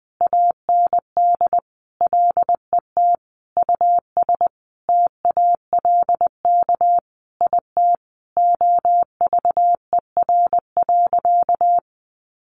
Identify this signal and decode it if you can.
Morse code, heard as a keyed tone, about 19 wpm, 700 Hz. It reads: AND LET US TALK IT OVER.
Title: Sentences from Top 500 Words 20wpm